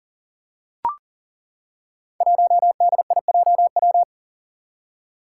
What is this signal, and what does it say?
Morse code, heard as a keyed tone, about 40 wpm, 700 Hz. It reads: 1DIJW
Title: Five Challenging Letter-Number Combinations 40wpm